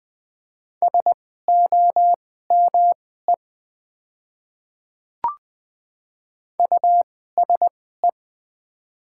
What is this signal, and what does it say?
Morse code, heard as a keyed tone, about 20 wpm, spaced as for 14 wpm, 690 Hz. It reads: SOME USE
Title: 2x Character Spacing - Top 100 Words - 20wpm